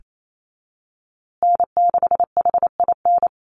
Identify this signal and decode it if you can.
Morse code, heard as a keyed tone, about 28 wpm, 700 Hz. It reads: N6HIN
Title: USA Call Signs Repeated 2x 28wpm